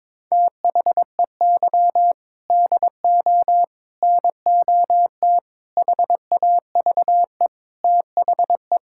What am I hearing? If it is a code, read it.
Morse code, heard as a keyed tone, about 22 wpm, 700 Hz. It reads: THEY DO NOT HAVE THE
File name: practice / 22wpm / Build A Phrase - 5-Word Phrases - 22wpm